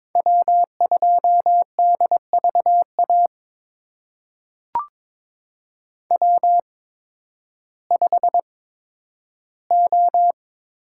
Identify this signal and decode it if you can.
Morse code, heard as a keyed tone, about 22 wpm, spaced as for 12 wpm, 700 Hz. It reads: W2DVA W5O
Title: USA Call Signs - 5 Characters - 8x Character Spacing - 22wpm